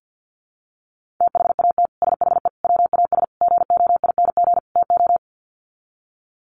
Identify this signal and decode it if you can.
Morse code, heard as a keyed tone, about 50 wpm, 700 Hz. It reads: THAT SHE WAS GOING TO
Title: Sets of 5 Words Encore2 50wpm